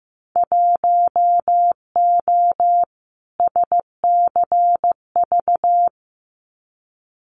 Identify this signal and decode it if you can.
Morse code, heard as a keyed tone, about 15 wpm, 700 Hz. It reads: 1O SCV
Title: Winter Field Day Exchanges 15wpm